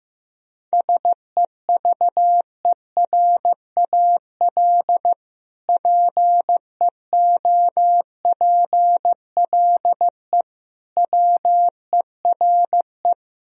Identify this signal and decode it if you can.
Morse code, heard as a keyed tone, about 16 wpm, 700 Hz. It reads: SEVERAL PEOPLE WERE
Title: Sentences from Top 1000 Words - Review Part 2 - 15wpm